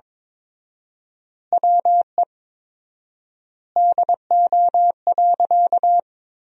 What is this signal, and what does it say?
Morse code, heard as a keyed tone, about 21 wpm, 700 Hz. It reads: WE DO.
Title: 4x Word-Spacing - Sentences from Top 100 Words - 22wpm